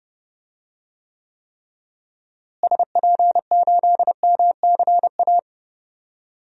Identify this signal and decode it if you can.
Morse code, heard as a keyed tone, about 30 wpm, 700 Hz. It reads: SP8MCA
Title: DXCC-100 Call Signs - 6 Characters - Repeated 3x - 30wpm